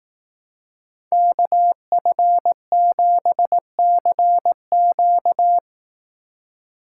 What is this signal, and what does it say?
Morse code, heard as a keyed tone, about 18 wpm, 700 Hz. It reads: KF7CQ
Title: Speed Racing - USA Call Signs 15wpm